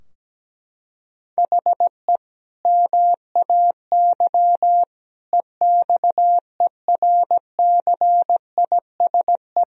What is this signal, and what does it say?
Morse code, heard as a keyed tone, about 17 wpm, 700 Hz. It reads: HE MAY EXERCISE
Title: Sentences from Top 1000 Words - Review Part 4 - 17wpm